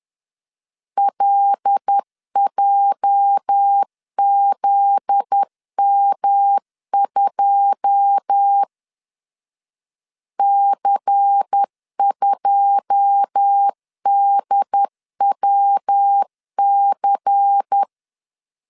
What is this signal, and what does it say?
Morse code, heard as a keyed tone, about 10 wpm, 800 Hz. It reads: LJZM2 C2DWC
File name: LEZIONI / 9 2 10wpm 2736